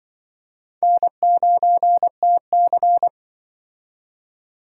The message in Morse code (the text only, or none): N9TC